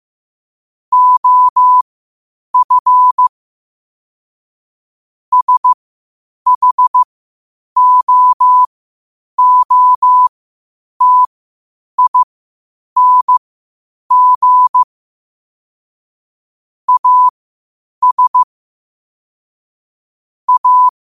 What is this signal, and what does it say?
Morse code, heard as a keyed tone, about 15 wpm, spaced as for 8 wpm, 1000 Hz. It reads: OF SHOOTING AS A
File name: qotdfiles / 8wpm